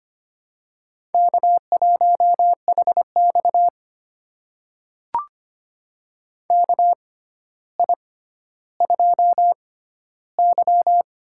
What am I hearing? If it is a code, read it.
Morse code, heard as a keyed tone, about 25 wpm, spaced as for 13 wpm, 700 Hz. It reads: K1HX KI2Y
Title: USA Call Signs - 4 Characters - 6x Character Spacing - 25wpm